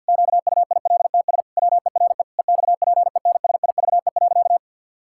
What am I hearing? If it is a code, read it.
Morse code, heard as a keyed tone, about 50 wpm, 700 Hz. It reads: QUILTS WERE EXPENSIVE.
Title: Pangram Challenge v2 - 50wpm